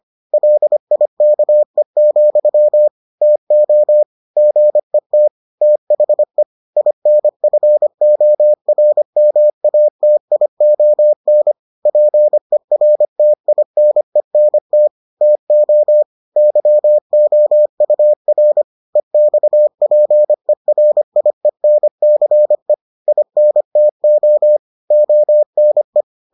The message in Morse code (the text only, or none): LIKE, TO GET THE INFORMATION PERTINENT TO YOUR EXPERIENCE INTO ONE